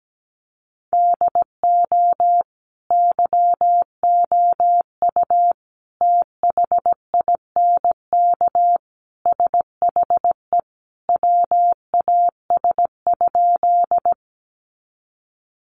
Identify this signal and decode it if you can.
Morse code, heard as a keyed tone, about 17 wpm, 700 Hz. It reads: DO YOU THINK SHE WAS?
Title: Sentences from Top 200 Words 17wpm